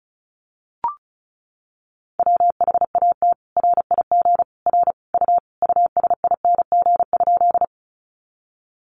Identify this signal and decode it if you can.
Morse code, heard as a keyed tone, about 34 wpm, 700 Hz. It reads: WHAT RIG R U USING?
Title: Common QSO Phrases 35wpm